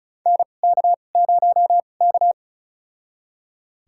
Morse code, heard as a keyed tone, about 36 wpm, spaced as for 25 wpm, 700 Hz. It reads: NK0K
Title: USA Call Signs - 2x Character Spacing - 35wpm